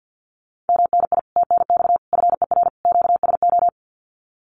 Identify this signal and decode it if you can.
Morse code, heard as a keyed tone, about 50 wpm, 700 Hz. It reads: MNI TNX FER QSO
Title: Common QSO Phrases Encore 50wpm